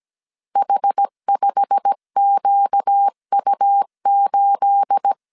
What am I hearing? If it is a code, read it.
Morse code, heard as a keyed tone, about 17 wpm, 800 Hz. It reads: H5QU8
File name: 1500C S B 16wpm